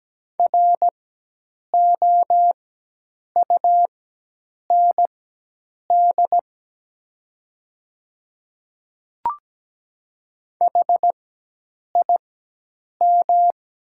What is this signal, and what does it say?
Morse code, heard as a keyed tone, about 17 wpm, spaced as for 8 wpm, 700 Hz. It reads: ROUND HIM
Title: 4x Character Spacing - Top 200 Words - 17wpm